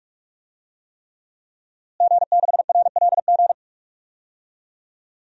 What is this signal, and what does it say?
Morse code, heard as a keyed tone, about 41 wpm, 700 Hz. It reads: G6RLG
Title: DXCC-100 Call Signs - 5 Characters - Repeated 3x - 45wpm